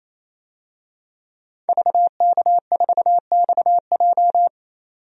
Morse code, heard as a keyed tone, about 27 wpm, 700 Hz. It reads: VK4XJ
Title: DXCC-100 Call Signs - 5 Characters - 28wpm